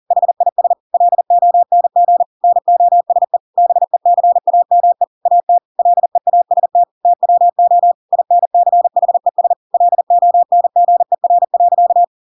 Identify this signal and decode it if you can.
Morse code, heard as a keyed tone, about 40 wpm, 700 Hz. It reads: HIS LONG NOSE BECAME AT LEAST TWO INCHES LONGER.
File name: Pinocchio / Pinocchio ch5 40 40 ik1hge